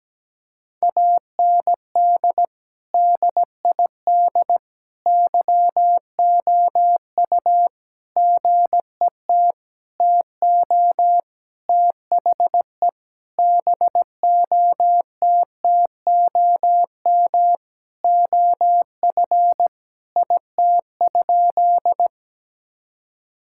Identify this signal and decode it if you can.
Morse code, heard as a keyed tone, about 17 wpm, 700 Hz. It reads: AND DID YOU GET TO THE BOTTOM OF IT?
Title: Sentences from Top 1000 Words - Review Part 6 - 17wpm